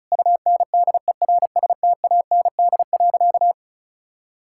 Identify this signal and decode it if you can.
Morse code, heard as a keyed tone, about 34 wpm, 700 Hz. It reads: UNDERSTAND.
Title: Sentences from Top 1000 Words - Review Part 5 - 35wpm